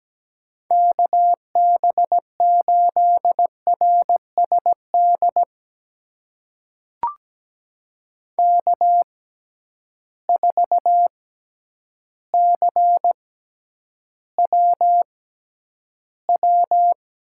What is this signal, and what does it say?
Morse code, heard as a keyed tone, about 17 wpm, spaced as for 9 wpm, 700 Hz. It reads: KB8RSD K4CWW
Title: USA Call Signs - 6x Character Spacing - 17wpm